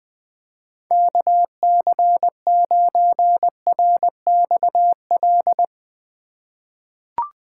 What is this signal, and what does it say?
Morse code, heard as a keyed tone, about 20 wpm, 700 Hz. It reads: KC9RXL
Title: USA Call Signs - 6 Characters - 2x Character Spacing - 20wpm